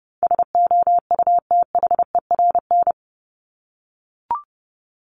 Morse code, heard as a keyed tone, about 30 wpm, 700 Hz. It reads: SOUTHERN
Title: Top 1751-2000 Words 30wpm